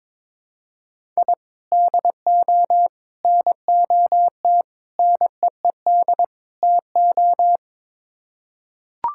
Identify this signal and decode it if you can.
Morse code, heard as a keyed tone, about 22 wpm, 700 Hz. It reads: I DO NOT NEED TO
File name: practice / 22wpm / Build A Phrase - 5-Word Phrases - 22wpm